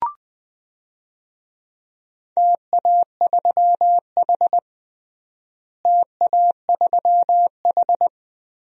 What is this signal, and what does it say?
Morse code, heard as a keyed tone, about 20 wpm, 700 Hz. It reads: TA3H TA3H
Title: DXCC-100 Call Signs - 4 Characters - Repeated 2x - 20wpm